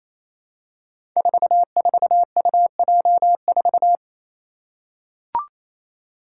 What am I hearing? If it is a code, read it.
Morse code, heard as a keyed tone, about 28 wpm, 700 Hz. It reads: 44UJ4
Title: Five Challenging Letter-Number Combinations 28wpm